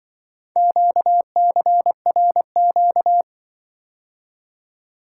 Morse code, heard as a keyed tone, about 24 wpm, 700 Hz. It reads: QCRQ
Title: Speed-Racing - Four Letter-Number Combinations - 20wpm